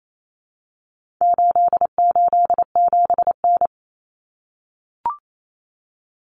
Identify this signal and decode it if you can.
Morse code, heard as a keyed tone, about 28 wpm, 700 Hz. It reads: 887N
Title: Common Cut Numbers - 4-Digit 28wpm